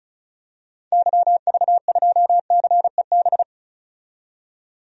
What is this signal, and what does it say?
Morse code, heard as a keyed tone, about 34 wpm, 700 Hz. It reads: YV2CEB
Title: DXCC-100 Call Signs - 6 Characters - 35wpm